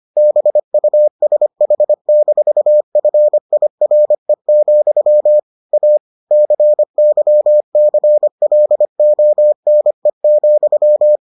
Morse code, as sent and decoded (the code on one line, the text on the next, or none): -... ..- ... .... -....- ..-. .. .-. --..-- / .- / -.-. -.-- -.-. .-.. --- -. --..--
BUSH-FIRE, A CYCLONE,